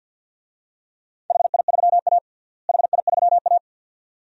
Morse code, heard as a keyed tone, about 53 wpm, 700 Hz. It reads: HI3A HI3A
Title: DXCC-100 Call Signs - 4 Characters - Repeated 2x - 50wpm